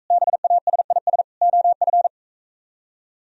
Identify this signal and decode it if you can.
Morse code, heard as a keyed tone, about 45 wpm, 700 Hz. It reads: BASIS OF